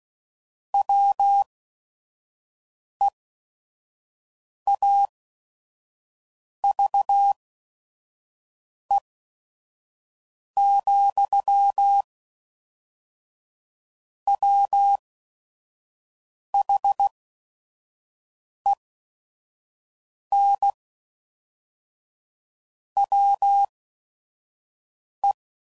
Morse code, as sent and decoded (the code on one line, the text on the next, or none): .-- . .- ...- . --..-- / .-- .... -. / .-- .
WEAVE, WHEN WE